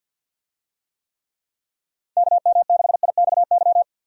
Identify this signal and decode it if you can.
Morse code, heard as a keyed tone, about 48 wpm, 700 Hz. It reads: KM6IXY
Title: USA Call Signs - 6 Characters - 50wpm